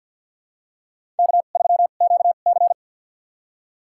Morse code, heard as a keyed tone, about 49 wpm, spaced as for 35 wpm, 700 Hz. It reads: K3QC
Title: USA Call Signs - 4 Characters - 2x Character Spacing - 50wpm